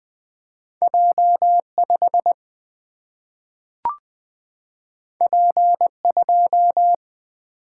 Morse code, heard as a keyed tone, about 20 wpm, 700 Hz. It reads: J5 P2
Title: Single-Letter Single-Number 20wpm